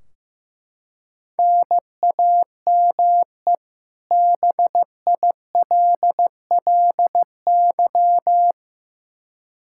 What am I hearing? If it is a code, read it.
Morse code, heard as a keyed tone, about 15 wpm, 700 Hz. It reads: NAME BILLY
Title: Practice QSO - Exchange 1 - Beginning - 15wpm